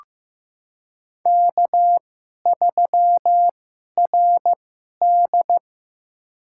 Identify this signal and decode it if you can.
Morse code, heard as a keyed tone, about 15 wpm, spaced as for 11 wpm, 700 Hz. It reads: K3RD
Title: USA Call Signs - 4 Characters - 2x Character Spacing - 15wpm